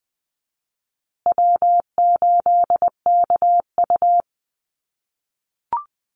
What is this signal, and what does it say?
Morse code, heard as a keyed tone, about 20 wpm, 700 Hz. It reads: W8KU
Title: USA Call Signs Repeated 2x 20wpm